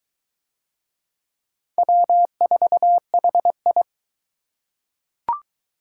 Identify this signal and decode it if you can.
Morse code, heard as a keyed tone, about 21 wpm, 700 Hz. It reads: W4HI